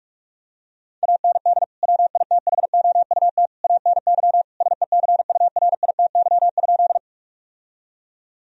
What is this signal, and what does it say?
Morse code, heard as a keyed tone, about 46 wpm, 700 Hz. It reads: AND WITHOUT ANY SECURITY?
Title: Sentences from Top 1000 Words - Review Part 4 - 45wpm